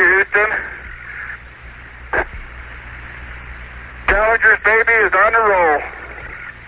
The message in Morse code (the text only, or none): none